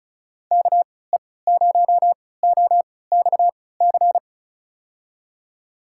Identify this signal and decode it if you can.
Morse code, heard as a keyed tone, about 34 wpm, spaced as for 20 wpm, 700 Hz. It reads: KE0OXC
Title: USA Call Signs - 6 Characters - 3x Character Spacing - 35wpm